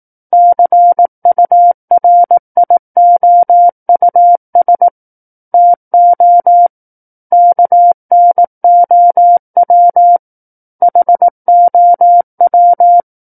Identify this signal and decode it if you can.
Morse code, heard as a keyed tone, about 18 wpm, 700 Hz. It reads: CURIOUS TO KNOW HOW
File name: 18 WPM